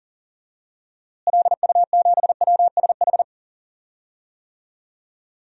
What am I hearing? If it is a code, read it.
Morse code, heard as a keyed tone, about 40 wpm, 700 Hz. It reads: LU7WSH
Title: DXCC-100 Call Signs - 6 Characters - Repeated 3x - 40wpm